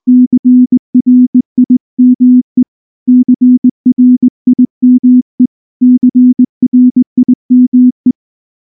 Morse code, heard as a keyed tone, about 19 wpm, 250 Hz. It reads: CRIME CRIME CRIME
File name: morse / morse-training-20WPM-2